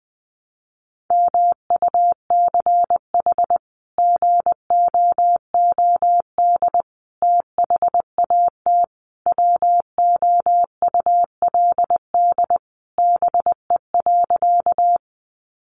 Morse code, heard as a keyed tone, about 20 wpm, 700 Hz. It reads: MUCH GOOD THAT WOULD BE.
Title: Sentences from Top 200 Words 20wpm